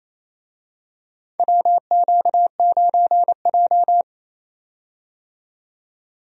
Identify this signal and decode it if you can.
Morse code, heard as a keyed tone, about 28 wpm, 700 Hz. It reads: WQ9J
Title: USA Call Signs - 4 Characters - 28wpm